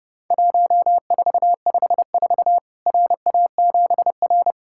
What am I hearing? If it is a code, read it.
Morse code, heard as a keyed tone, about 30 wpm, 700 Hz. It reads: 1454 RA7R
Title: Worked All Europe DX Contest - QTC Exchange - 30wpm